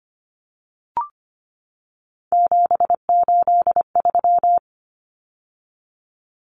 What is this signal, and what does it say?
Morse code, heard as a keyed tone, about 25 wpm, 700 Hz. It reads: 783
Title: Common Cut Numbers - 3-Digit 25wpm